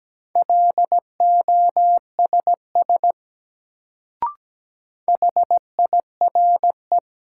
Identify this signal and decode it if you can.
Morse code, heard as a keyed tone, about 17 wpm, 700 Hz. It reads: LOSS HIRE
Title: Sentences from Top 1400 Words - New Word Drill Alt - 17wpm